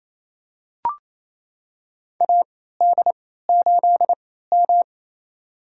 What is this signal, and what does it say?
Morse code, heard as a keyed tone, about 28 wpm, spaced as for 16 wpm, 700 Hz. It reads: AD8M
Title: USA Call Signs - 4 Characters - 3x Character Spacing - 28wpm